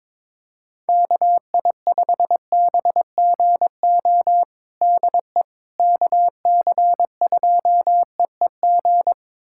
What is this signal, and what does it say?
Morse code, heard as a keyed tone, about 22 wpm, 700 Hz. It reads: KI5BGO DE KC2EEG
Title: USA Callsign DE Callsign 22wpm